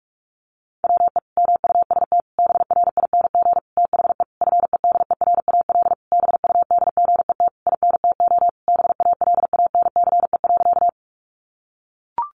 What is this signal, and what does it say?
Morse code, heard as a keyed tone, about 44 wpm, 700 Hz. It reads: WE MUST BRING THE FEDERAL BUDGET INTO BALANCE.
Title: Sentences from Top 1200 Words 45wpm